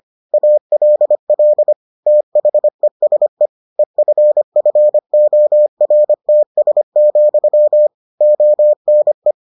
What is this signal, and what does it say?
Morse code, heard as a keyed tone, about 25 wpm, 600 Hz. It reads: ALL THESE EFFORTS, ONE